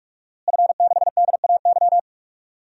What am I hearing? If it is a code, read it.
Morse code, heard as a keyed tone, about 41 wpm, 700 Hz. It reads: F6DAY
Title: DXCC-100 Call Signs - 5 Characters - Repeated 3x - 45wpm